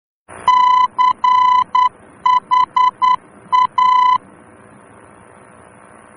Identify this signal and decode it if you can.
Morse code, heard as a keyed tone, about 10 wpm, 1000 Hz. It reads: CHA